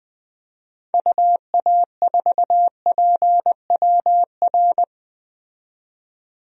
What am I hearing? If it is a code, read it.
Morse code, heard as a keyed tone, about 20 wpm, 700 Hz. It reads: UA4PWR
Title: DXCC-100 Call Signs - 6 Characters - 20wpm